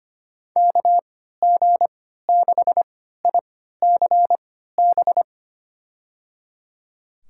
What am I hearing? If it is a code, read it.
Morse code, heard as a keyed tone, about 25 wpm, spaced as for 14 wpm, 700 Hz. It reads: KG6ICB
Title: USA Call Signs - 6 Characters - 3x Character Spacing - 25wpm